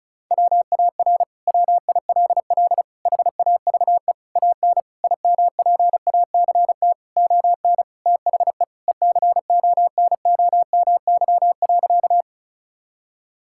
Morse code, heard as a keyed tone, about 34 wpm, 700 Hz. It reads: WAR WILL HAVE AN IMPACT ON THE ECONOMY.
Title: Sentences from Top 1000 Words - Review Part 6 - 35wpm